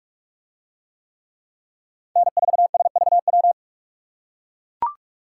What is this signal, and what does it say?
Morse code, heard as a keyed tone, about 44 wpm, 700 Hz. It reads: N4SVW